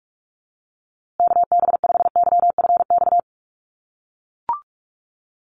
Morse code, heard as a keyed tone, about 44 wpm, 690 Hz. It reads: KB5YFX
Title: USA Call Signs Repeated 3x 45wpm